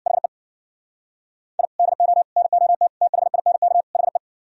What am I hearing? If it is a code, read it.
Morse code, heard as a keyed tone, about 64 wpm, 690 Hz. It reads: HE I DO NOT THINK HE